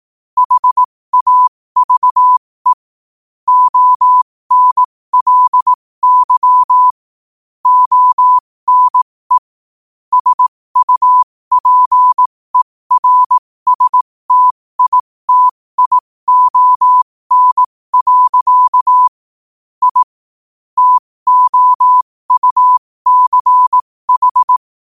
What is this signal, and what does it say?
Morse code, heard as a keyed tone, about 18 wpm, spaced as for 15 wpm, 1000 Hz. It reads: HAVE ONLY ONE SUPERSTITION. I TOUCH
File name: qotdfiles / 15wpm